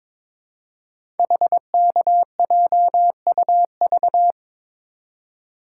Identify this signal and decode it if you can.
Morse code, heard as a keyed tone, about 22 wpm, 700 Hz. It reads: HKJUV
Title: Code Groups - 5 Letters - 22wpm